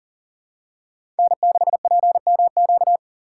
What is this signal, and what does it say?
Morse code, heard as a keyed tone, about 40 wpm, 700 Hz. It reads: N6PMQ
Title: USA Call Signs - 5 Characters - 2x Character Spacing - 40wpm